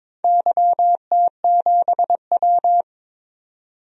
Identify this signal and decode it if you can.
Morse code, heard as a keyed tone, about 21 wpm, 700 Hz. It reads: YT7W
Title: DXCC-100 Call Signs - 4 Characters - 22wpm